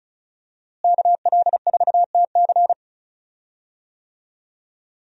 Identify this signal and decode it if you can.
Morse code, heard as a keyed tone, about 36 wpm, 700 Hz. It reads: KL4TC
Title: USA Call Signs - 5 Characters - Repeated 3x - 35wpm